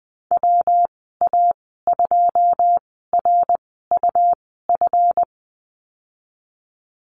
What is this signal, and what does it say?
Morse code, heard as a keyed tone, about 20 wpm, spaced as for 15 wpm, 700 Hz. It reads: WA2RUF
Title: 173 Lesson 28g - Number 9 - Call Signs 20wpm